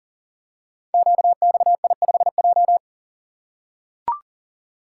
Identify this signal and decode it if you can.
Morse code, heard as a keyed tone, about 41 wpm, 690 Hz. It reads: QXI5J
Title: Code Groups - 5 Letter:Numbers - 40wpm